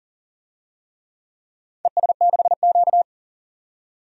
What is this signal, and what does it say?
Morse code, heard as a keyed tone, about 42 wpm, 700 Hz. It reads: ES6Q
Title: DXCC-100 Call Signs - 4 Characters - 40wpm